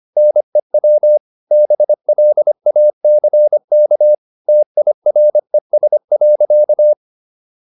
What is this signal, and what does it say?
Morse code, heard as a keyed tone, about 25 wpm, 600 Hz. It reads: NEW BLACK TIRES.